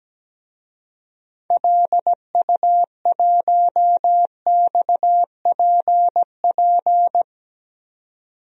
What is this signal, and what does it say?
Morse code, heard as a keyed tone, about 17 wpm, 700 Hz. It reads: LU1XPP
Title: DXCC-100 Call Signs - 6 Characters - 17wpm